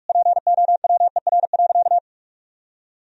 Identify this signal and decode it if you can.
Morse code, heard as a keyed tone, about 45 wpm, 700 Hz. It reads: POWER.
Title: Sentences from Top 1000 Words - Review Part 2 - 45wpm